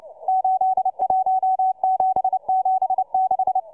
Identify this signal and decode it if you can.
Morse code, heard as a keyed tone, about 29 wpm, 750 Hz, with radio noise. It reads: 81776